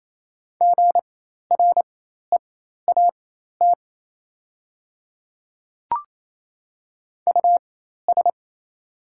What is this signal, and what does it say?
Morse code, heard as a keyed tone, about 27 wpm, spaced as for 13 wpm, 700 Hz. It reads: GREAT US
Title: 4x Character Spacing - Top 300 Words - 28wpm